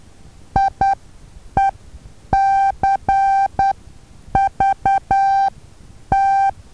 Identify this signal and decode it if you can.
Morse code, heard as a keyed tone, about 10 wpm, spaced as for 8 wpm, 780 Hz. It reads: IECVT
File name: Taviro HA6VA A